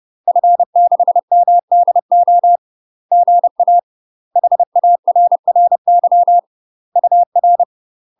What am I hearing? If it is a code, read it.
Morse code, heard as a keyed tone, about 30 wpm, 700 Hz. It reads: F6MDO GA HARRY UR
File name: UK MASTER QSOs TRAINING / 30WPM / UK MASTER QSOs 0036